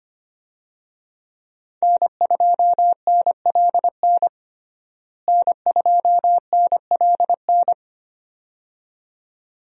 Morse code, heard as a keyed tone, about 25 wpm, 700 Hz. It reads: N2NLN N2NLN
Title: USA Call Signs - 5 Characters - Repeated 2x - 25wpm